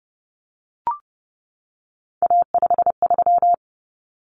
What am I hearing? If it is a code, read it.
Morse code, heard as a keyed tone, about 30 wpm, 700 Hz. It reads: A53
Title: Common Cut Numbers - 3-Digit 30wpm